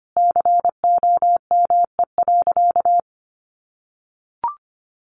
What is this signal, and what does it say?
Morse code, heard as a keyed tone, about 25 wpm, 700 Hz. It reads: COME.
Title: Sentences from Top 1100 Words 25wpm